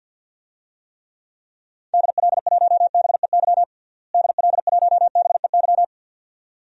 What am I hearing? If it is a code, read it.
Morse code, heard as a keyed tone, about 52 wpm, 700 Hz. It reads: DL1BEY DL1BEY
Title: DXCC-100 Call Signs - 6 Characters - Repeated 2x - 50wpm